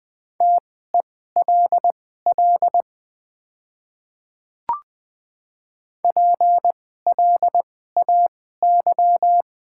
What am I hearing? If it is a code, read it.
Morse code, heard as a keyed tone, about 21 wpm, spaced as for 15 wpm, 700 Hz. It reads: TELL PLAY
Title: 2x Character Spacing - Top 200 Words - 20wpm